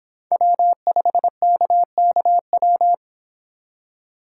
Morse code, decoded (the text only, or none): W5KKW